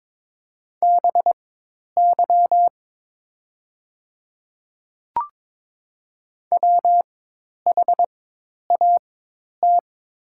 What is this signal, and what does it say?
Morse code, heard as a keyed tone, about 22 wpm, spaced as for 10 wpm, 700 Hz. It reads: BY WHAT
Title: 4x Character Spacing - Top 100 Words - 22wpm